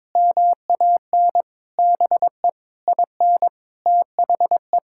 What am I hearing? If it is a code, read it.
Morse code, heard as a keyed tone, about 22 wpm, 700 Hz. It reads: MAN BE IN THE
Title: Sentences from Top 1000 Words - Review Part 6 - 22wpm